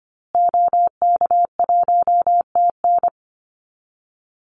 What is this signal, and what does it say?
Morse code, heard as a keyed tone, about 25 wpm, 700 Hz. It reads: OK1TN
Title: Call Signs Repeated 2x - 25wpm